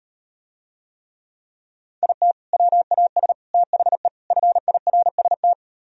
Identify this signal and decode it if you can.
Morse code, heard as a keyed tone, about 38 wpm, 690 Hz. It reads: IT WAS THE FIRST